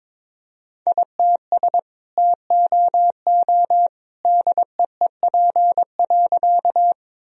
Morse code, heard as a keyed tone, about 22 wpm, 700 Hz. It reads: ITS TOO DEEP.
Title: Sentences from Top 1000 Words - Review Part 3 - 22wpm